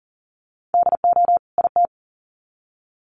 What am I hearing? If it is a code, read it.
Morse code, heard as a keyed tone, about 40 wpm, 700 Hz. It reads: DO IT
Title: Sets of 2 Words 40wpm